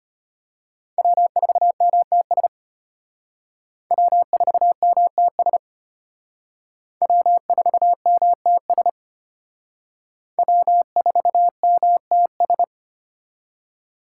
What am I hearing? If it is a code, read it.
Morse code, heard as a keyed tone, about 32 wpm, 700 Hz. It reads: W4MTS W4MTS W4MTS W4MTS